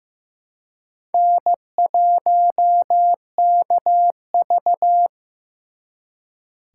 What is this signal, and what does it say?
Morse code, heard as a keyed tone, about 18 wpm, 700 Hz. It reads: N1KV
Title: Speed Racing - USA 1x2 Call Signs 15wpm